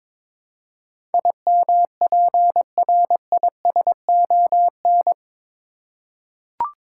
Morse code, heard as a keyed tone, about 21 wpm, 700 Hz. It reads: IMPRISON
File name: practice / 22wpm / Eight Letter Words - 22wpm